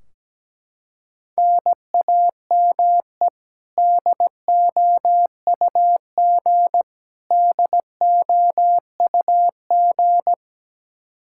Morse code, heard as a keyed tone, about 17 wpm, 700 Hz. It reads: NAME DOUG DOUG
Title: Practice QSO - Exchange 1 - Beginning - 17wpm